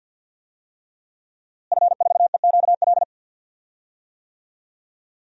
Morse code, heard as a keyed tone, about 51 wpm, 700 Hz. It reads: F4EQL
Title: DXCC-100 Call Signs - 5 Characters - Repeated 3x - 50wpm